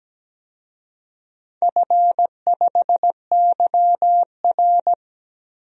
Morse code, heard as a keyed tone, about 17 wpm, 700 Hz. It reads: F5YR
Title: DXCC-100 Call Signs - 4 Characters - Repeated 2x - 17wpm